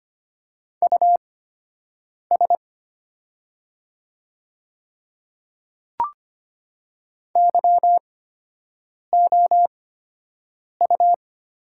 Morse code, heard as a keyed tone, about 25 wpm, spaced as for 7 wpm, 700 Hz. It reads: US YOU
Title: 8x Character Spacing - Top 200 Words - 25wpm